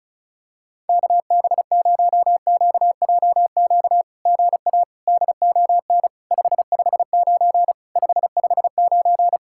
Morse code, heard as a keyed tone, about 33 wpm, 700 Hz. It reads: KB0QJQ GA DON 559 559
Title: SOTA QSO Practice 35wpm